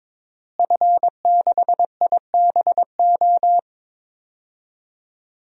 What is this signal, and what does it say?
Morse code, heard as a keyed tone, about 22 wpm, 700 Hz. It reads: F6IBO
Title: DXCC-100 Call Signs - 5 Characters - Repeated 2x - 22wpm